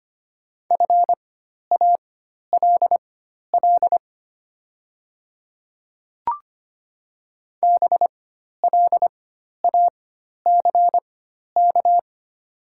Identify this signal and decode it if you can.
Morse code, heard as a keyed tone, about 25 wpm, spaced as for 12 wpm, 700 Hz. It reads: FALL BLACK
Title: 4x Character Spacing - Top 500 Words - 25wpm